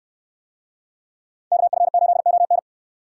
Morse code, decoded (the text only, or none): KF8ON